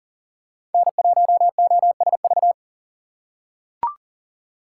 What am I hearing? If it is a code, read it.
Morse code, heard as a keyed tone, about 40 wpm, 700 Hz. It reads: N1OSV
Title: USA Call Signs - 5 Characters - 2x Character Spacing - 40wpm